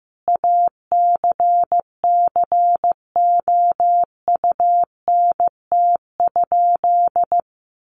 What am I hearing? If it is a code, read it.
Morse code, heard as a keyed tone, about 15 wpm, 700 Hz. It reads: ACCOUNT?
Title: Sentences from Top 1500 Words 15wpm